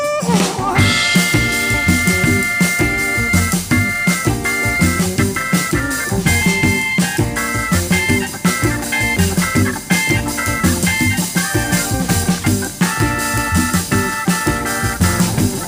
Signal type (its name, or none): dial-up modem handshake